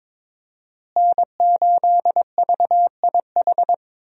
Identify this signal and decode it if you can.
Morse code, heard as a keyed tone, about 22 wpm, 700 Hz. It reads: N8VIH
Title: USA Call Signs - 5 Characters - Repeated 2x - 22wpm